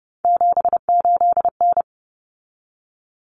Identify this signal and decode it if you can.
Morse code, heard as a keyed tone, about 29 wpm, 700 Hz. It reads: 78N
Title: Common Cut Numbers - 3-Digit 30wpm